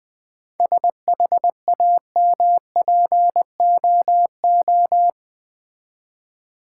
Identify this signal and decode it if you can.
Morse code, heard as a keyed tone, about 20 wpm, 700 Hz. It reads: SHAMPOO
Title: Easy Words with Hints - 20wpm